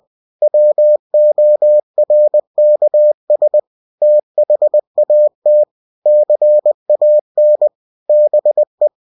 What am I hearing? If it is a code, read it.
Morse code, heard as a keyed tone, about 20 wpm, 600 Hz. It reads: WORKS THAT CAN BE